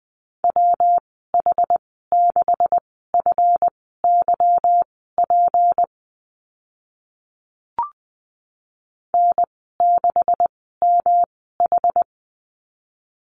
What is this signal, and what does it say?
Morse code, heard as a keyed tone, about 20 wpm, spaced as for 14 wpm, 700 Hz. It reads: WH6FYP N6MH
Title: 161 Lesson 26g - Letter g - Call Signs 20wpm